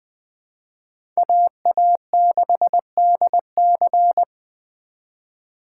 Morse code, heard as a keyed tone, about 21 wpm, 700 Hz. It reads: AA6DC
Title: USA Call Signs - 5 Characters - 4x Character Spacing - 20wpm